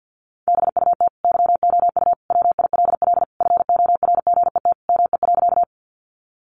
Morse code, heard as a keyed tone, about 50 wpm, 700 Hz. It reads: BUT YOU WILL FORGET ME.
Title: Sentences from Top 1200 Words 50wpm